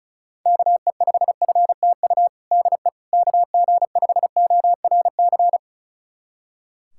Morse code, heard as a keyed tone, about 36 wpm, 700 Hz. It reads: KE5FTU DE KG5ORC
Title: USA Callsign DE Callsign 35wpm